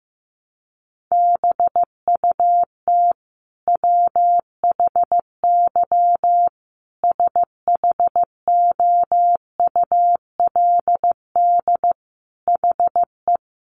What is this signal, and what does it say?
Morse code, heard as a keyed tone, about 15 wpm, 700 Hz. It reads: BUT WHY SHOULD HE
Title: Sentences from Top 1200 Words 15wpm